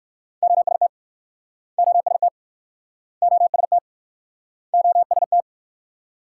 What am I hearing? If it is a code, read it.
Morse code, heard as a keyed tone, about 60 wpm, 700 Hz. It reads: OST OST OST OST